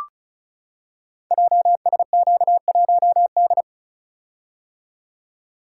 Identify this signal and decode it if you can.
Morse code, heard as a keyed tone, about 36 wpm, 690 Hz. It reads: JSQ1D
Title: Code Groups - 5 Letter:Numbers - 35wpm